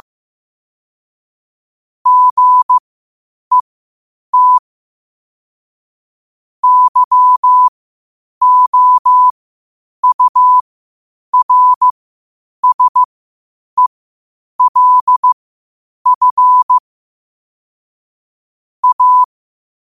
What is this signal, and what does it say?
Morse code, heard as a keyed tone, about 15 wpm, spaced as for 8 wpm, 1000 Hz. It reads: GET YOURSELF A
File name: qotdfiles / 8wpm